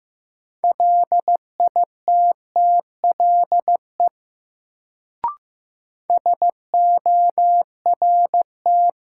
Morse code, heard as a keyed tone, about 15 wpm, 700 Hz. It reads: LITTLE SORT
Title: Sets of 2 Words - New Word Drill - 15wpm